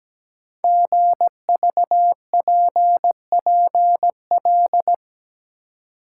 Morse code, heard as a keyed tone, about 17 wpm, 700 Hz. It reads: GVPPL
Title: Code Groups - 5 Letters - 17wpm